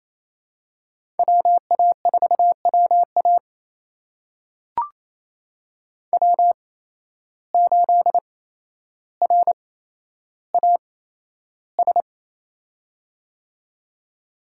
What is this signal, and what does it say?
Morse code, heard as a keyed tone, about 28 wpm, spaced as for 12 wpm, 700 Hz. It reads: WA4WA W8RAS
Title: USA Call Signs - 5 Characters - 8x Character Spacing - 28wpm